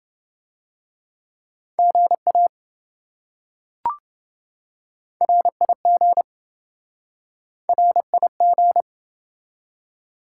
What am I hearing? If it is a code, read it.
Morse code, heard as a keyed tone, about 27 wpm, 700 Hz. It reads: GA RIG RIG